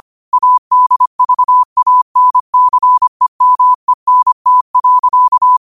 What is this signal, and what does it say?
Morse code, heard as a keyed tone, about 25 wpm, 1000 Hz. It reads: ADVANCEMENT.
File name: qotdfiles / 25wpm